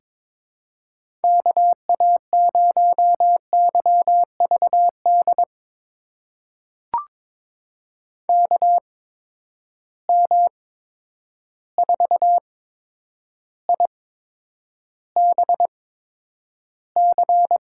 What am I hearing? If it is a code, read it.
Morse code, heard as a keyed tone, about 22 wpm, spaced as for 9 wpm, 700 Hz. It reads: KA0YVD KM4IBC
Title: USA Call Signs - 6 Characters - 8x Character Spacing - 22wpm